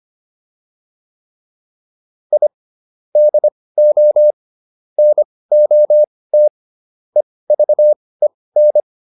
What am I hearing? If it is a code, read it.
Morse code, heard as a keyed tone, about 25 wpm, spaced as for 18 wpm, 600 Hz. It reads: I DO NOT EVEN